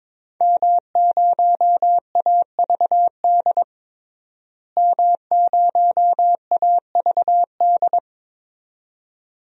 Morse code, heard as a keyed tone, about 22 wpm, 700 Hz. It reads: M0AVD M0AVD
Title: DXCC-100 Call Signs - 5 Characters - Repeated 2x - 22wpm